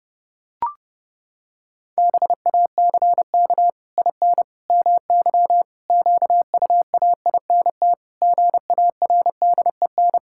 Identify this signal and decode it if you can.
Morse code, heard as a keyed tone, about 30 wpm, 700 Hz. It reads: BACK IN MY QUAINT GARDEN
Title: Pangram Challenge v2 - 30wpm